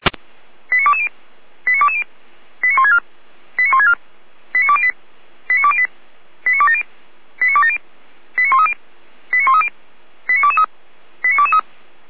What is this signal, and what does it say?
Five-tone selective call (ZVEI), heard as quick runs of five tones, tones of about 80 ms. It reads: 89220 89220 89164 89164 89208 89208 89270 89270 89130 89130 89223 89223
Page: /455/SDR/ZVEI/89110/